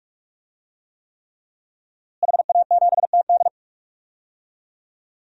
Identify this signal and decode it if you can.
Morse code, heard as a keyed tone, about 46 wpm, 700 Hz. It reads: HA7TD
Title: DXCC-100 Call Signs - 5 Characters - Repeated 3x - 45wpm